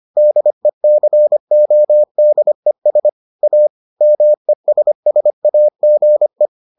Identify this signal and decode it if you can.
Morse code, heard as a keyed tone, about 25 wpm, 600 Hz. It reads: DECODES A MESSAGE